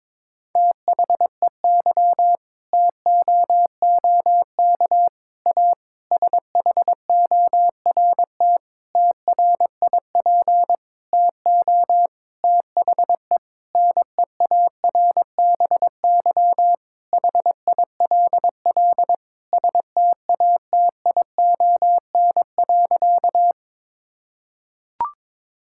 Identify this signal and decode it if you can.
Morse code, heard as a keyed tone, about 22 wpm, 700 Hz. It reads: THEY TOOK A SHORT TRIP TO THE NEARBY HILL STATION.
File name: Difficult Sentences with Hints - 22wpm